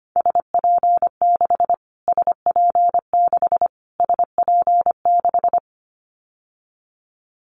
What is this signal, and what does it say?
Morse code, heard as a keyed tone, about 25 wpm, 700 Hz. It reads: SP6 SP6 SP6
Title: Callsign Prefixes for Top 25 DXCC - Repeated 3x - 25wpm